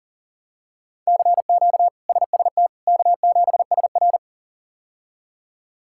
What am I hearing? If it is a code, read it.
Morse code, heard as a keyed tone, about 40 wpm, 700 Hz. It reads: CQ SST K7SR
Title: K1USN SST QSO Practice - 40wpm